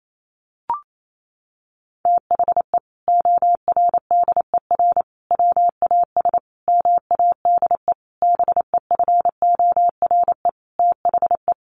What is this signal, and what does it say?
Morse code, heard as a keyed tone, about 28 wpm, 700 Hz. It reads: THE ORDER WAS MADE BEFORE THE
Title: Sentences from Top 1500 Words 28wpm